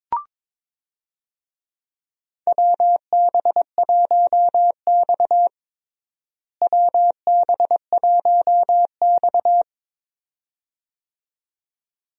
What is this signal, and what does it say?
Morse code, heard as a keyed tone, about 22 wpm, 700 Hz. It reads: WB1X WB1X
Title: USA Call Signs - 4 Characters - Repeated 2x - 22wpm